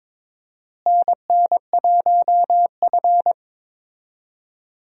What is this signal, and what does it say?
Morse code, heard as a keyed tone, about 22 wpm, 700 Hz. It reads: NN1F
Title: USA Call Signs - 4 Characters - Repeated 2x - 22wpm